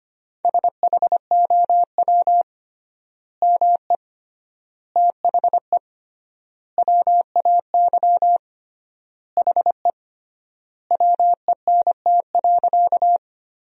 Morse code, heard as a keyed tone, about 25 wpm, 700 Hz. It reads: SHOW ME THE WAY HE WENT.
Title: 3x Word-Spacing - Sentences from Top 200 Words - 25wpm